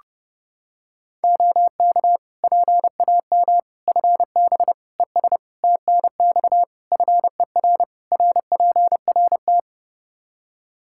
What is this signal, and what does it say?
Morse code, heard as a keyed tone, about 30 wpm, 700 Hz. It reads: OK PAM FB ES TNX FER RPRT
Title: K1USN SST QSO Practice 30wpm